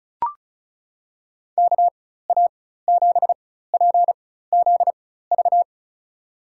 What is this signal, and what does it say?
Morse code, heard as a keyed tone, about 34 wpm, spaced as for 16 wpm, 700 Hz. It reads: KA7PZV
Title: USA Call Signs - 6 Characters - 4x Character Spacing - 35wpm